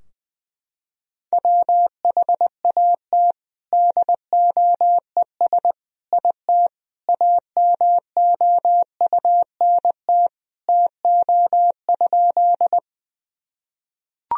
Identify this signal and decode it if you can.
Morse code, heard as a keyed tone, about 19 wpm, 700 Hz. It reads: WHAT DOES IT AMOUNT TO?
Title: Sentences from Top 1000 Words - Review Part 4 - 20wpm